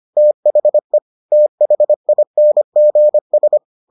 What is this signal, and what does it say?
Morse code, heard as a keyed tone, about 25 wpm, 600 Hz. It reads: THE THINGS